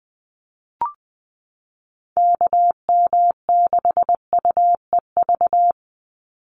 Morse code, heard as a keyed tone, about 20 wpm, 700 Hz. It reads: KM6UEV
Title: USA Call Signs 20wpm